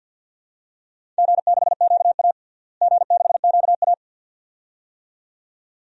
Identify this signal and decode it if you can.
Morse code, heard as a keyed tone, about 52 wpm, 700 Hz. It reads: G6QA G6QA
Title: DXCC-100 Call Signs - 4 Characters - Repeated 2x - 50wpm